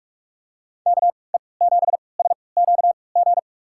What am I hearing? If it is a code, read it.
Morse code, heard as a keyed tone, about 43 wpm, spaced as for 25 wpm, 700 Hz. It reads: KE7SQG
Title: USA Call Signs - 6 Characters - 3x Character Spacing - 45wpm